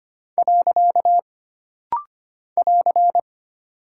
Morse code, heard as a keyed tone, about 25 wpm, 700 Hz. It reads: . <AR>
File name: Single Punctuation-Prosign - Rapid-Fire - 25wpm